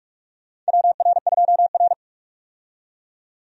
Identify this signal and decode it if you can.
Morse code, heard as a keyed tone, about 46 wpm, 700 Hz. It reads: WR2R
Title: USA Call Signs - 4 Characters - 45wpm